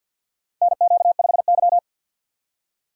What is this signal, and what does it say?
Morse code, heard as a keyed tone, about 50 wpm, 700 Hz. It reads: NQ5Y
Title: USA Call Signs - 4 Characters - 2x Character Spacing - 50wpm